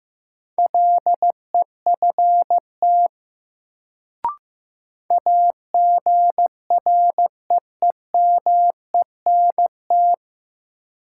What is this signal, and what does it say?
Morse code, heard as a keyed tone, about 16 wpm, 700 Hz. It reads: LEFT AGREEMENT
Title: Sets of 5 Words Encore - New Word Drill - 15wpm